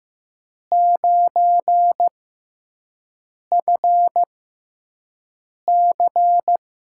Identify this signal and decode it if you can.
Morse code, heard as a keyed tone, about 15 wpm, spaced as for 5 wpm, 700 Hz. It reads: 9FC
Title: USA Call Signs - 4 Characters - 6x Character Spacing - 15wpm